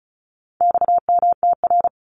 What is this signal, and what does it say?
Morse code, heard as a keyed tone, about 34 wpm, 700 Hz. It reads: XMTR
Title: Abbreviations 35wpm